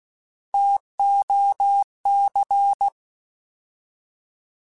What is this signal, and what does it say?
Morse code, heard as a keyed tone, about 16 wpm, 800 Hz. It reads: TOC